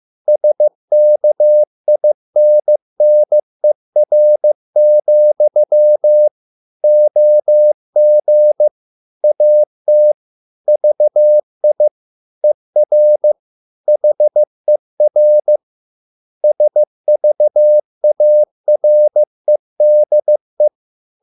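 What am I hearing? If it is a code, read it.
Morse code, heard as a keyed tone, about 15 wpm, 600 Hz. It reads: SKINNER, OG AT VI ER HER SVAREDE
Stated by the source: Morse code at a slow 15 wpm from CW med Gnister.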